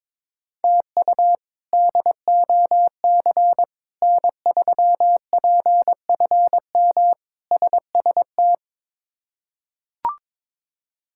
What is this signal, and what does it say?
Morse code, heard as a keyed tone, about 22 wpm, 700 Hz. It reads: TU DOC N3PFM SST
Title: K1USN SST QSO Practice - 22wpm